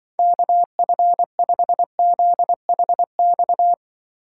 Morse code, heard as a keyed tone, about 24 wpm, 700 Hz. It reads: KF5ZHX
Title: Speed Racing - USA 2x3 Call Signs 20wpm